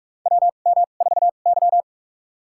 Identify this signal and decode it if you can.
Morse code, heard as a keyed tone, about 44 wpm, spaced as for 32 wpm, 700 Hz. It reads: WM4Y
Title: USA Call Signs - 4 Characters - 2x Character Spacing - 45wpm